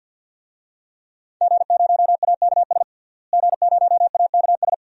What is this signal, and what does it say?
Morse code, heard as a keyed tone, about 51 wpm, 700 Hz. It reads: G0AKS G0AKS
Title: DXCC-100 Call Signs - 5 Characters - Repeated 2x - 50wpm